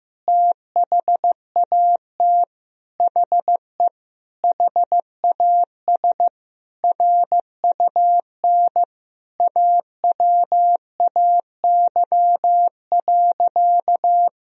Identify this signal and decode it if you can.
Morse code, heard as a keyed tone, about 15 wpm, 700 Hz. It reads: THAT HE HAS RUN AWAY.
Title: Sentences from Top 1000 Words - Review Part 1 - 15wpm